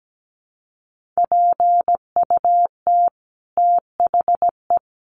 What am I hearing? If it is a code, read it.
Morse code, heard as a keyed tone, about 17 wpm, 700 Hz. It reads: PUT THE
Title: Sentences from Top 1400 Words 17wpm